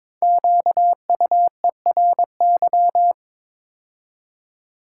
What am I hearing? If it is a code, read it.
Morse code, heard as a keyed tone, about 22 wpm, 700 Hz. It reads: QUERY
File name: 5 Letter Words - First 3 Letters Given - 22wpm